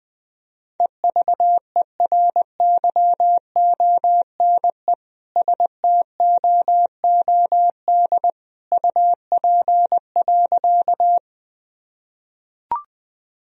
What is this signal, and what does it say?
Morse code, heard as a keyed tone, about 20 wpm, 700 Hz. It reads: EVERYONE STOOD UP.
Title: Sentences from Top 1000 Words - Review Part 2 - 20wpm